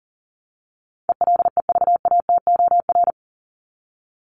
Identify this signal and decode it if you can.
Morse code, heard as a keyed tone, about 40 wpm, 700 Hz. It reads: ELEVATOR
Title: Top 1501-1750 Words 40wpm